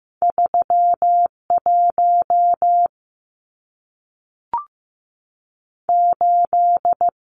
Morse code, heard as a keyed tone, about 15 wpm, 700 Hz. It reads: 31 8
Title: Numbers - 2-Digit 15wpm